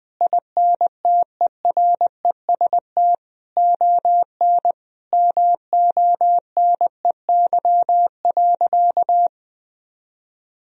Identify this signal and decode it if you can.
Morse code, heard as a keyed tone, about 19 wpm, 700 Hz. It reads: INTEREST ON MONEY.
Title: Sentences from Top 1000 Words - Review Part 4 - 20wpm